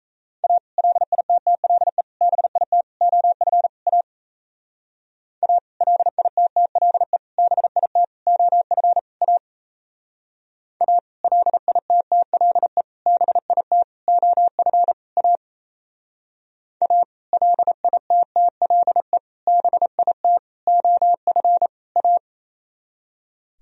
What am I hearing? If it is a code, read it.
Morse code, heard as a keyed tone, about 35 wpm, 700 Hz. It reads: A LITTLE BIT OF A A LITTLE BIT OF A A LITTLE BIT OF A A LITTLE BIT OF A
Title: Speed-Racing - Sets of 5 Words 28wpm